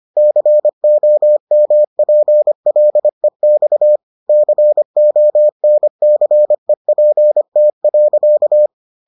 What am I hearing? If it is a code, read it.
Morse code, heard as a keyed tone, about 25 wpm, 600 Hz. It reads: COMPLEX CONCEPT.